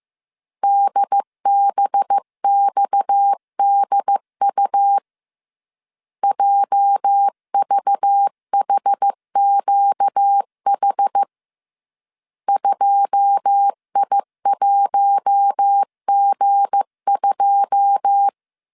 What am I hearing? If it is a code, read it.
Morse code, heard as a keyed tone, about 15 wpm, 800 Hz. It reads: DBXDU JVHQH 2I1G2
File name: CW / DET / 1 9 1500C S C 14wpm 3245